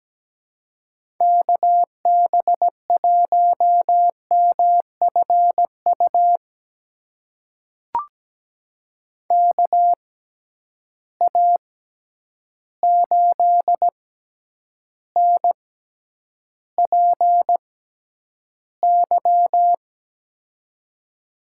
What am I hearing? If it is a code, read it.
Morse code, heard as a keyed tone, about 17 wpm, spaced as for 9 wpm, 700 Hz. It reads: KB1MFU KA8NPY
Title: USA Call Signs - 6 Characters - 6x Character Spacing - 17wpm